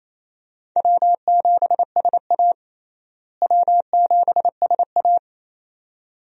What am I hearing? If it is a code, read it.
Morse code, heard as a keyed tone, about 28 wpm, 700 Hz. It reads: W7SA W7SA
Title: USA Call Signs - 4 Characters - Repeated 2x - 28wpm